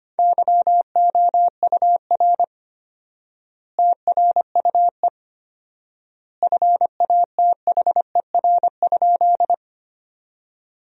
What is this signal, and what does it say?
Morse code, heard as a keyed tone, about 25 wpm, 700 Hz. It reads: YOUR TRUE FATHER?
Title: 4x Word-Spacing - Sentences from Top 400 Words - 25wpm